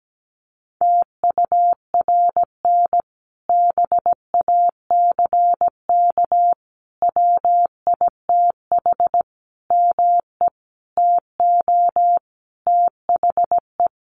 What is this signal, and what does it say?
Morse code, heard as a keyed tone, about 17 wpm, 700 Hz. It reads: TURN BACK WITH ME TO THE
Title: Sentences from Top 1300 Words 17wpm